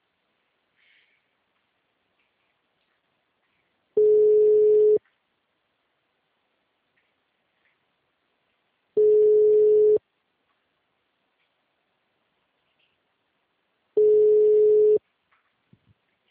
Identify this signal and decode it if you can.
Ringback tone (Europe, 425 Hz).
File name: out- -2010- - - .1368